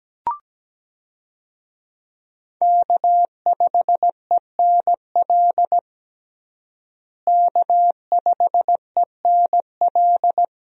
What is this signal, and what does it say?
Morse code, heard as a keyed tone, about 17 wpm, 700 Hz. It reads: K5ENL K5ENL
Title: USA Call Signs - 5 Characters - Repeated 3x - 17wpm